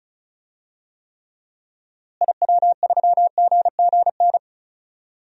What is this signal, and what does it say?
Morse code, heard as a keyed tone, about 34 wpm, 700 Hz. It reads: IW3GGN
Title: DXCC-100 Call Signs - 6 Characters - Repeated 2x - 35wpm